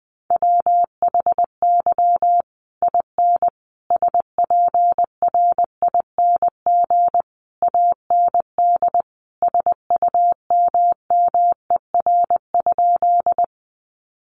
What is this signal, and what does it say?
Morse code, heard as a keyed tone, about 20 wpm, 700 Hz. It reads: WHY IN SPRING AND SUMMER?
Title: Sentences from Top 1200 Words 20wpm